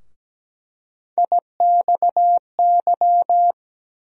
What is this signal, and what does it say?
Morse code, heard as a keyed tone, about 17 wpm, 700 Hz. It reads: IXY
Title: USA Call Sign Postfixes - 3 Letters 17wpm